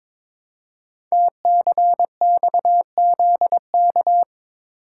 Morse code, heard as a keyed tone, about 22 wpm, 700 Hz. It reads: TCXZK
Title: Code Groups - 5 Letters - 22wpm